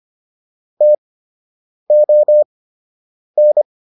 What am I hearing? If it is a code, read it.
Morse code, heard as a keyed tone, about 25 wpm, spaced as for 8 wpm, 600 Hz. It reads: TON